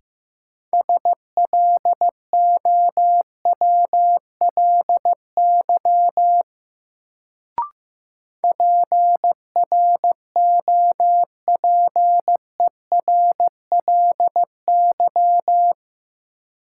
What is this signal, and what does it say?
Morse code, heard as a keyed tone, about 15 wpm, 700 Hz. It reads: SLOWLY PROPERLY
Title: Sentences from Top 1100 Words - New Word Drill - 15wpm